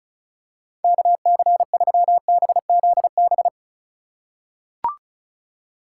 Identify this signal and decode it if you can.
Morse code, heard as a keyed tone, about 36 wpm, 700 Hz. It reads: KC3BZB
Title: USA Call Signs - 6 Characters - 3x Character Spacing - 35wpm